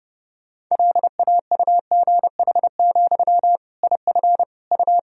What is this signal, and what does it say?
Morse code, heard as a keyed tone, about 30 wpm, 700 Hz. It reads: LAUGH, IF U
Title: Sentences from Top 500 Words and Abbreviations 30wpm